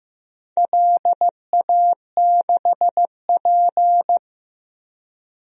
Morse code, heard as a keyed tone, about 15 wpm, 700 Hz. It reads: LA6P
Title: DXCC-100 Call Signs - 4 Characters - 15wpm